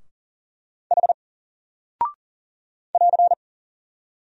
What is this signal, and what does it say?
Morse code, heard as a keyed tone, about 38 wpm, 700 Hz. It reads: H <AR>
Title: Single Character (Letter:Number:Punctuation:Prosign) 40wpm